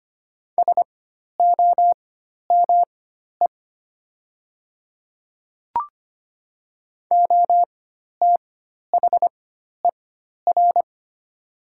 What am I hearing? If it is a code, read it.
Morse code, heard as a keyed tone, about 25 wpm, spaced as for 12 wpm, 700 Hz. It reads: SOME OTHER
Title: 4x Character Spacing - Top 100 Words - 25wpm